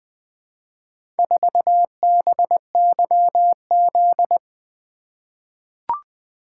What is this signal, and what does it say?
Morse code, heard as a keyed tone, about 20 wpm, 700 Hz. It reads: 4BYZ
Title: Four Letter-Number Combinations - 20wpm